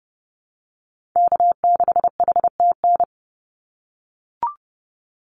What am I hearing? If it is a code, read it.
Morse code, heard as a keyed tone, about 30 wpm, 700 Hz. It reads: K6HTN
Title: Call Signs - 30wpm